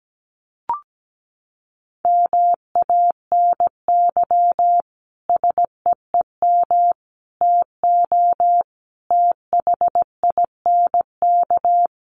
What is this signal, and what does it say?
Morse code, heard as a keyed tone, about 17 wpm, 700 Hz. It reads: MANY SEEM TO THINK
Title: Sentences from Top 1400 Words 17wpm